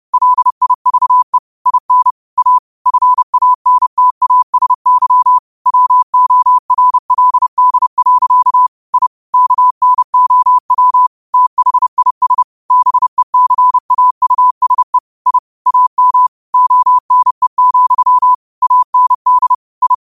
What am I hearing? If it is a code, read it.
Morse code, heard as a keyed tone, about 30 wpm, 1000 Hz. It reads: LIVE IN A FANTASY WORLD. I KNOW THIS BECAUSE I AM ONE, AND I
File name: qotdfiles / 30wpm